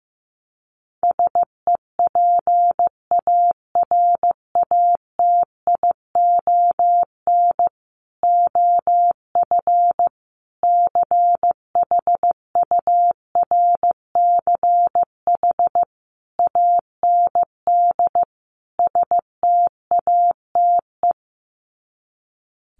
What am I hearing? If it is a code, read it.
Morse code, heard as a keyed tone, about 15 wpm, 700 Hz. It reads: SEPARATION OF CHURCH AND STATE
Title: Sets of 5 Words 15wpm